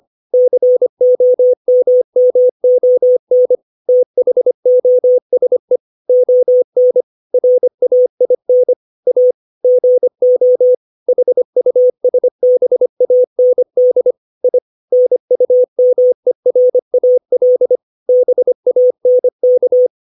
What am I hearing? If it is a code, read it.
Morse code, heard as a keyed tone, about 25 wpm, 500 Hz. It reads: COMMON THOSE ON RAIN A GO HUSBAND I NUMERAL BANK